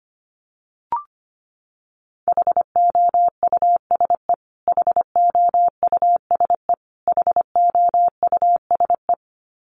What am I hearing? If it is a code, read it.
Morse code, heard as a keyed tone, about 25 wpm, 700 Hz. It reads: HOUSE HOUSE HOUSE
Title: Top 300 Words Repeated 25wpm